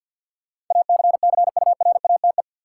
Morse code, heard as a keyed tone, about 52 wpm, 700 Hz. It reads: ACCURATE
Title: Sentences from Top 1200 Words - New Word Drill - 50wpm